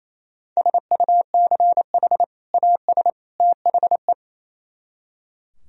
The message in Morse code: SUCH AS THE

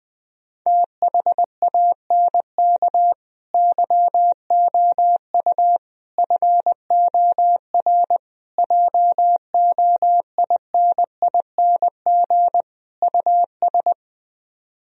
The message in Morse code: THANK YOU FOR JOINING US

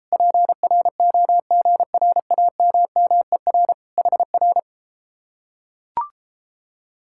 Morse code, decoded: PROGRAMMER HR

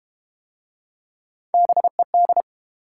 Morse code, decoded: BED

